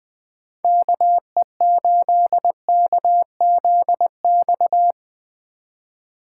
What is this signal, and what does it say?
Morse code, heard as a keyed tone, about 19 wpm, 700 Hz. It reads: KE8KZX
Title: Speed Racing - USA 2x3 Call Signs 17wpm